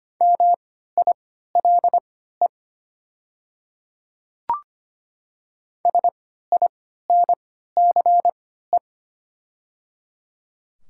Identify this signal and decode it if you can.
Morse code, heard as a keyed tone, about 25 wpm, spaced as for 14 wpm, 700 Hz. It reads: MILE SINCE
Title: 3x Character Spacing - Top 300 Words - 25wpm